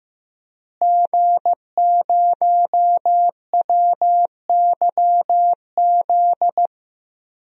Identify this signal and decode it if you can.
Morse code, heard as a keyed tone, about 15 wpm, 700 Hz. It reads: G0WYZ
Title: DXCC-100 Call Signs - 5 Characters - Repeated 3x - 15wpm